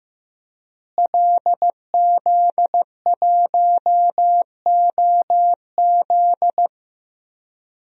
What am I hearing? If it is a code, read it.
Morse code, heard as a keyed tone, about 15 wpm, 700 Hz. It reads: LZ1OZ
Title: DXCC-100 Call Signs - 5 Characters - Repeated 2x - 15wpm